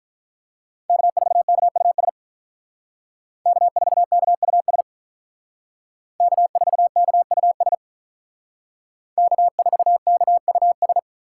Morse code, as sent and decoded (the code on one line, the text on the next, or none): -.- ....- -.- ..- ... / -.- ....- -.- ..- ... / -.- ....- -.- ..- ... / -.- ....- -.- ..- ...
K4KUS K4KUS K4KUS K4KUS